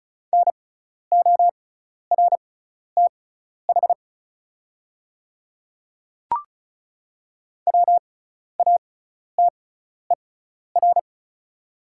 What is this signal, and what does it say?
Morse code, heard as a keyed tone, about 36 wpm, spaced as for 12 wpm, 690 Hz. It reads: NORTH WATER
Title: 6x Character Spacing - Top 500 Words - 35wpm